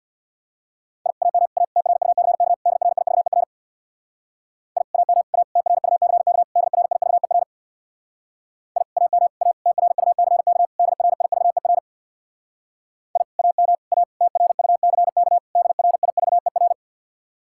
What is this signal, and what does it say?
Morse code, heard as a keyed tone, about 66 wpm, 700 Hz. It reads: I AM A TRUCK DRIVER I AM A TRUCK DRIVER I AM A TRUCK DRIVER I AM A TRUCK DRIVER